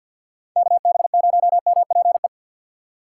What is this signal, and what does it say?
Morse code, heard as a keyed tone, about 52 wpm, 700 Hz. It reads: KB0MPE
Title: USA Call Signs - 3x Character Spacing - 50wpm